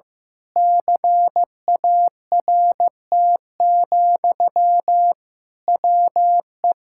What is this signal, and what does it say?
Morse code, heard as a keyed tone, about 15 wpm, 700 Hz. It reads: CART, WE
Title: Pangram Challenge v2 - 15wpm